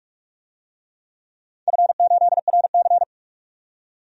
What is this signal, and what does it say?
Morse code, heard as a keyed tone, about 44 wpm, 700 Hz. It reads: F8RC
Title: DXCC-100 Call Signs - 4 Characters - 45wpm